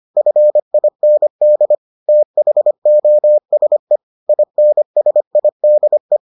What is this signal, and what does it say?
Morse code, heard as a keyed tone, about 25 wpm, 600 Hz. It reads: FIND THOSE INSIDE